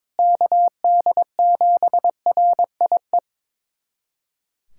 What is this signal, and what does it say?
Morse code, heard as a keyed tone, about 22 wpm, 700 Hz. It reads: KD7RIE
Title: USA Callsign DE Callsign - 6 Characters - 22wpm